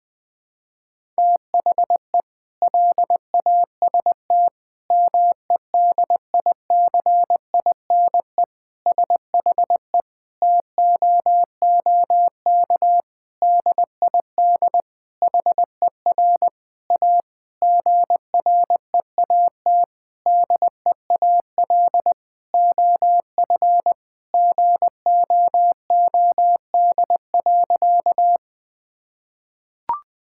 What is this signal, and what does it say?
Morse code, heard as a keyed tone, about 20 wpm, 700 Hz. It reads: THE LAST MEDICINE SHE TOOK DID HER A GREAT DEAL OF GOOD.
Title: Sentences from Top 1000 Words - Review Part 6 - 20wpm